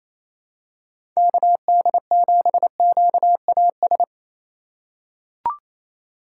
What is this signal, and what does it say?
Morse code, heard as a keyed tone, about 27 wpm, 700 Hz. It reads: KD7QAS
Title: USA Call Signs - 6 Characters - 8x Character Spacing - 28wpm